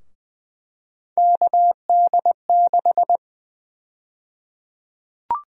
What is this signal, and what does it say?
Morse code, heard as a keyed tone, about 20 wpm, 700 Hz. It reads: KD6
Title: USA Call Sign Prefixes - 20wpm